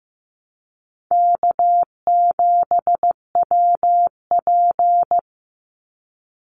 Morse code, heard as a keyed tone, about 15 wpm, 700 Hz. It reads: K7WP
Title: Easy Call Signs Repeated 2x - 15wpm